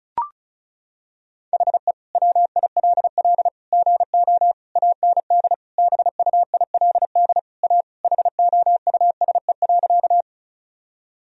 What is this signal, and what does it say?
Morse code, heard as a keyed tone, about 34 wpm, 700 Hz. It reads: HE WILL GO AND BUILD A HOUSE.
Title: Sentences from Top 1000 Words - Review Part 1 - 35wpm